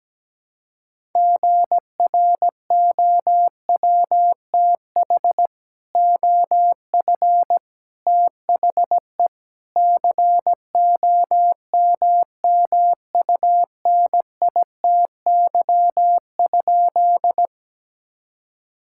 Morse code, heard as a keyed tone, about 17 wpm, 700 Hz. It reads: GROWTH OF THE COMMUNITY?
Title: Sentences from Top 1000 Words - Review Part 5 - 17wpm